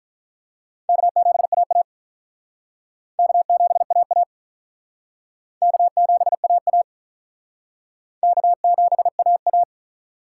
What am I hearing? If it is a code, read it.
Morse code, heard as a keyed tone, about 43 wpm, 700 Hz. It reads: K7AA K7AA K7AA K7AA
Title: Speed Racing - USA Call Signs 35wpm